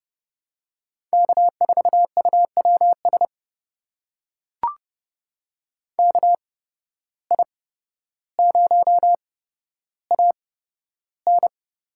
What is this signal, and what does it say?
Morse code, heard as a keyed tone, about 30 wpm, spaced as for 13 wpm, 700 Hz. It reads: K4UWS KI0AN
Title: USA Call Signs - 8x Character Spacing - 30wpm